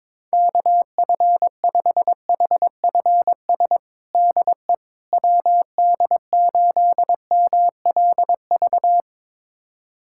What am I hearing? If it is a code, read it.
Morse code, heard as a keyed tone, about 21 wpm, 700 Hz. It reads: KF5HFS DE WD8MLV
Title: USA Callsign DE Callsign - 6 Characters - 22wpm